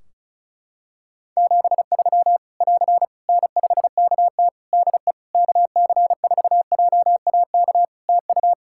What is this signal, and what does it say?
Morse code, heard as a keyed tone, about 34 wpm, 700 Hz. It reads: 73 <AR> N5KT DE KC4JAK TU
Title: K1USN SST QSO Practice 35wpm